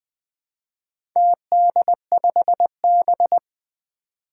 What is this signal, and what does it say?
Morse code, heard as a keyed tone, about 20 wpm, 700 Hz. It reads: TD5B
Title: Four Challenging Letter-Number Combinations 20wpm